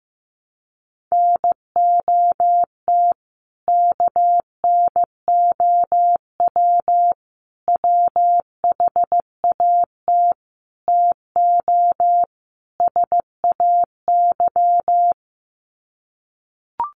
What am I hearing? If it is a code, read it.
Morse code, heard as a keyed tone, about 15 wpm, 700 Hz. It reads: NOT KNOW WHAT TO SAY
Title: Sets of 5 Words 15wpm